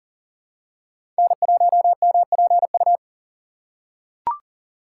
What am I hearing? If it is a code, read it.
Morse code, heard as a keyed tone, about 38 wpm, 700 Hz. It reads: N1MPU
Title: USA Call Signs - 5 Characters - 4x Character Spacing - 40wpm